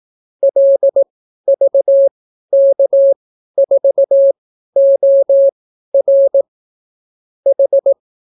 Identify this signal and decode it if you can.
Morse code, heard as a keyed tone, about 18 wpm, spaced as for 12 wpm, 550 Hz. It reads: LVK4OR H